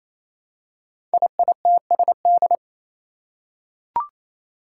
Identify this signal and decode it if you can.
Morse code, heard as a keyed tone, about 27 wpm, 700 Hz. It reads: IITSD
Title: Code Groups - 5 Letters - 28wpm